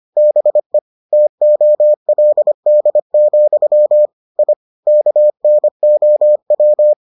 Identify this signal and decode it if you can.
Morse code, heard as a keyed tone, about 25 wpm, 600 Hz. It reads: BE TOLD, I KNOW